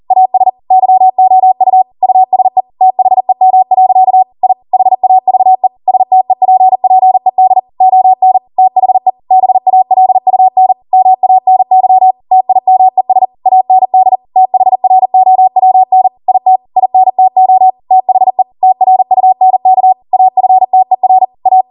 Morse code, heard as a keyed tone, about 40 wpm, 750 Hz. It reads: AS YOU USE THEM. I HAVE STEPPED ON THE BALUN MANY TIMES AND THROWN IT INTO THE TRUNK AFTER A